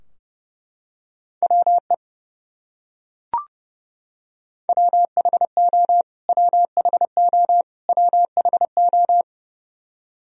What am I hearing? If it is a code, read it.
Morse code, heard as a keyed tone, about 32 wpm, 700 Hz. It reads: WE WHO WHO WHO
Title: Top 100 Words Repeated 30wpm